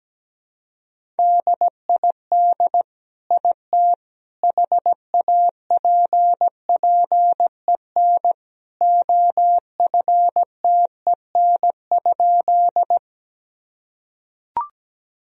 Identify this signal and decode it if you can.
Morse code, heard as a keyed tone, about 17 wpm, 700 Hz. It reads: DID IT HAPPEN OFTEN?
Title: Sentences from Top 1000 Words - Review Part 1 - 17wpm